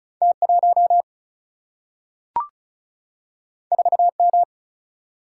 Morse code, heard as a keyed tone, about 36 wpm, 700 Hz. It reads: T1 4M
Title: Single-Letter Single-Number 35wpm